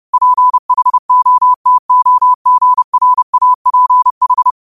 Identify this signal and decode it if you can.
Morse code, heard as a keyed tone, about 30 wpm, 1000 Hz. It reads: PHOTOGRAPH
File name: qotdfiles / 30wpm